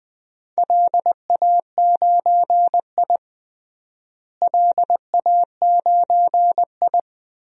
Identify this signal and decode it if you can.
Morse code, heard as a keyed tone, about 20 wpm, 700 Hz. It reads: LA9I LA9I
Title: DXCC-100 Call Signs - 4 Characters - Repeated 3x - 20wpm